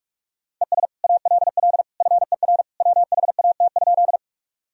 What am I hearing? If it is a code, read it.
Morse code, heard as a keyed tone, about 46 wpm, 700 Hz. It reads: ES ALL FER WHAT?
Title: Sentences from Top 100 Words and Abbreviations 45wpm